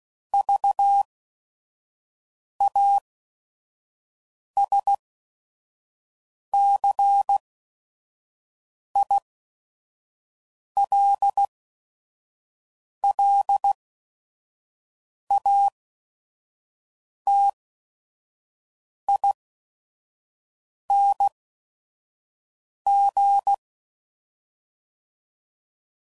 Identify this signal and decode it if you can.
Morse code, heard as a keyed tone, about 16 wpm, spaced as for 5 wpm, 800 Hz. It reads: VASCILLATING
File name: Suffixes / Special ING's / VerySlowToFaster ING's